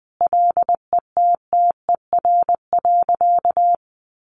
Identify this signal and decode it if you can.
Morse code, heard as a keyed tone, about 20 wpm, 700 Hz. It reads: LETTER.
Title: Sentences from Top 1400 Words 20wpm